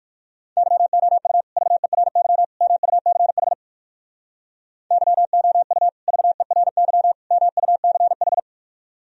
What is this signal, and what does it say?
Morse code, heard as a keyed tone, about 57 wpm, 700 Hz. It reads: YOU VERY MUCH YOU VERY MUCH